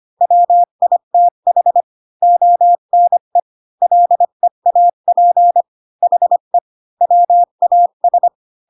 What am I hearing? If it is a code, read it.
Morse code, heard as a keyed tone, about 25 wpm, 700 Hz. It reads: WITH ONE LEAP HE WAS
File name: Pinocchio / Pinocchio ch3 20 25 ik1hge